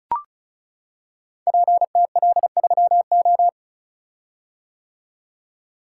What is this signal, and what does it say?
Morse code, heard as a keyed tone, about 34 wpm, 700 Hz. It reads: PTL3O
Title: Code Groups - 5 Letter:Numbers - 35wpm